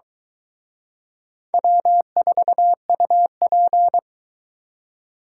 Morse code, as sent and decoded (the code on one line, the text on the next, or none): .-- ....- ..- .--.
W4UP